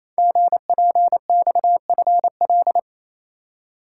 Morse code, heard as a keyed tone, about 28 wpm, 700 Hz. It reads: GPXFL
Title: Code Groups - 5 Letters - 28wpm